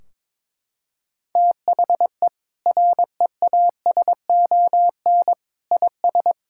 Morse code, heard as a keyed tone, about 22 wpm, 700 Hz. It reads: THE REASON IS
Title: Sentences from Top 1600 Words 22wpm